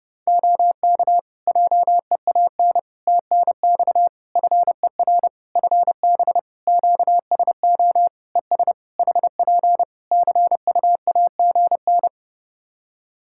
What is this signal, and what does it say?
Morse code, heard as a keyed tone, about 32 wpm, 700 Hz. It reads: OK JEAN TNX FER FB QSO ES HP CUAGN
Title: Practice QSO - Exchange 4 - Ending - 30wpm